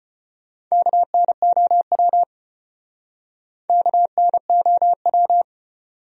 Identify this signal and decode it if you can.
Morse code, heard as a keyed tone, about 33 wpm, 700 Hz. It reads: KNOW KNOW